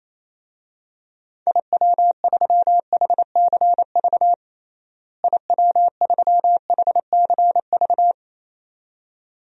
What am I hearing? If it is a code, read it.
Morse code, heard as a keyed tone, about 28 wpm, 700 Hz. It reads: IW3HCV IW3HCV
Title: DXCC-100 Call Signs - 6 Characters - Repeated 2x - 28wpm